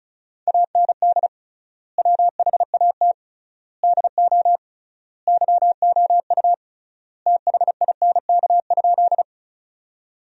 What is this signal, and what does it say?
Morse code, heard as a keyed tone, about 34 wpm, 700 Hz. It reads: AND WHAT DO YOU THINK?
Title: 3x Word-Spacing - Sentences from Top 200 Words - 35wpm